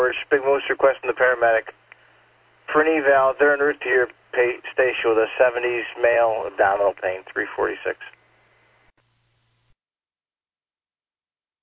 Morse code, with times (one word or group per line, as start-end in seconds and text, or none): none